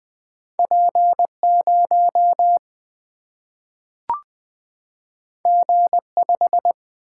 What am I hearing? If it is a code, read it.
Morse code, heard as a keyed tone, about 20 wpm, 700 Hz. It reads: P 0 G 5
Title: Single-Letter Single-Number 20wpm